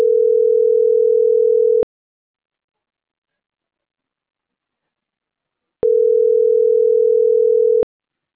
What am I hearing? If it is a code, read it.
Ringback tone (North America), 440 + 480 Hz.